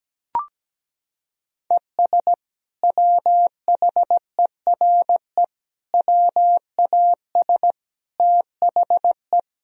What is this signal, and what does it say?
Morse code, heard as a keyed tone, about 17 wpm, 700 Hz. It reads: ES WHERE WAS THE
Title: Sentences from Top 200 Words and Abbreviations 17wpm